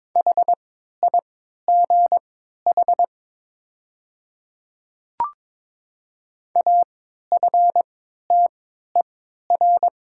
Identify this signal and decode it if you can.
Morse code, heard as a keyed tone, about 22 wpm, spaced as for 13 wpm, 700 Hz. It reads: HIGH AFTER
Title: 3x Character Spacing - Top 200 Words - 22wpm